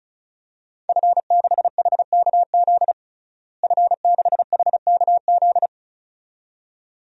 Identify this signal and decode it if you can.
Morse code, heard as a keyed tone, about 36 wpm, 700 Hz. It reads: F6HKZ F6HKZ
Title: DXCC-100 Call Signs - 5 Characters - Repeated 2x - 35wpm